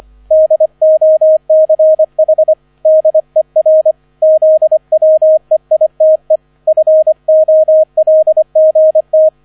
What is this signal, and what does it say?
Morse code, heard as a keyed tone, about 25 wpm, 630 Hz, with radio noise. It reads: DOCH DER ZWEITE FOLGT